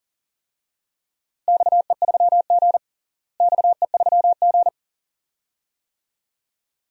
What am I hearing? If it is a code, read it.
Morse code, heard as a keyed tone, about 40 wpm, 700 Hz. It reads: XE3G XE3G
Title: DXCC-100 Call Signs - 4 Characters - Repeated 2x - 40wpm